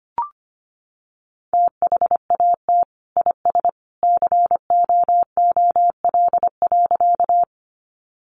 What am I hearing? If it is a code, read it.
Morse code, heard as a keyed tone, about 25 wpm, 700 Hz. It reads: THAT IS COOL.
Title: Sentences from Top 1200 Words 25wpm